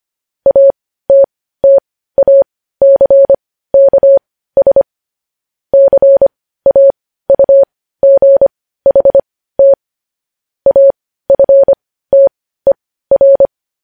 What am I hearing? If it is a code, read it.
Morse code, heard as a keyed tone, about 25 wpm, spaced as for 15 wpm, 550 Hz. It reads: ATTACKS CAUGHT AFTER
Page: News headlines at 15